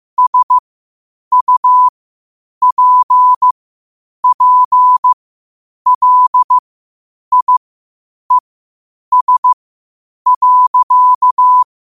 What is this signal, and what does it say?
Morse code, heard as a keyed tone, about 15 wpm, spaced as for 8 wpm, 1000 Hz. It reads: SUPPLIES.
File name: qotdfiles / 8wpm